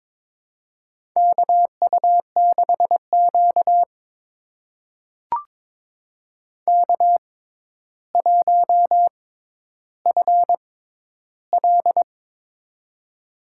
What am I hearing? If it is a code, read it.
Morse code, heard as a keyed tone, about 22 wpm, spaced as for 11 wpm, 700 Hz. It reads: KU6Q K1FL
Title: USA Call Signs - 4 Characters - 6x Character Spacing - 22wpm